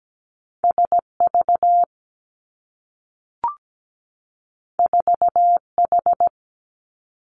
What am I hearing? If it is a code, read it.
Morse code, heard as a keyed tone, about 17 wpm, 700 Hz. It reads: SV 4H
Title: Two Challenging Letter-Number Combinations 17wpm